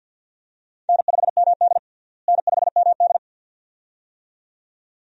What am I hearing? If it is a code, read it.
Morse code, heard as a keyed tone, about 48 wpm, 700 Hz. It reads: N5MD N5MD
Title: USA Call Signs - 4 Characters - Repeated 2x - 50wpm